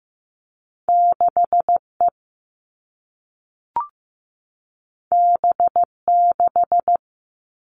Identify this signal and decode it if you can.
Morse code, heard as a keyed tone, about 15 wpm, 700 Hz. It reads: 6E B6
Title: All Cut Numbers - 2-Digit 15wpm